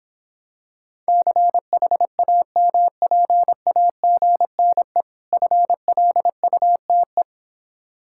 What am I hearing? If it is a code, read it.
Morse code, heard as a keyed tone, about 26 wpm, 700 Hz. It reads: CHAMPAGNE FLUTE